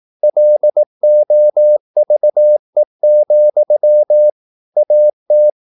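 Morse code, heard as a keyed tone, about 18 wpm, 600 Hz. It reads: LOVE, AT
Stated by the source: Slow (QRS) Morse practice at 18 wpm, CW med Gnister.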